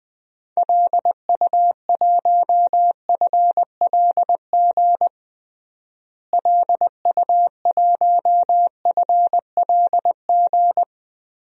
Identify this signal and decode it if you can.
Morse code, heard as a keyed tone, about 20 wpm, 700 Hz. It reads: LU1FLG LU1FLG
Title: DXCC-100 Call Signs - 6 Characters - Repeated 3x - 20wpm